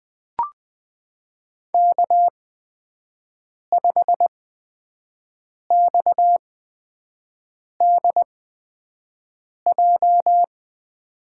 Morse code, heard as a keyed tone, about 21 wpm, spaced as for 5 wpm, 700 Hz. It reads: K5XDJ
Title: USA Call Signs - 8x Character Spacing - 20wpm